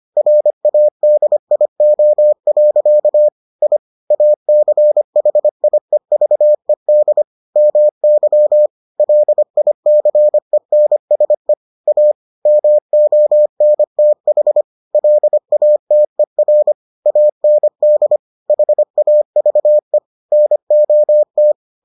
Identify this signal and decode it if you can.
Morse code, heard as a keyed tone, about 25 wpm, 600 Hz. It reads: RADIO. I ACHIEVED MY LICENSE A MONTH LATER AND HAVE NOT